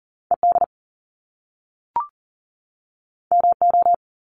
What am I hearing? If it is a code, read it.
Morse code, heard as a keyed tone, about 38 wpm, 700 Hz. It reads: ED MO
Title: Top Two Letter Combinations in Words 40wpm